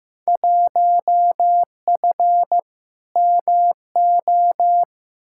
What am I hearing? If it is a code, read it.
Morse code, heard as a keyed tone, about 16 wpm, 700 Hz. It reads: 1F MO
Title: Field Day Exchanges Encore - 15wpm